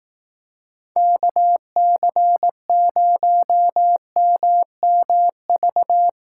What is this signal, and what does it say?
Morse code, heard as a keyed tone, about 18 wpm, 700 Hz. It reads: KC0MMV
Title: Speed Racing - USA 2x3 Call Signs 15wpm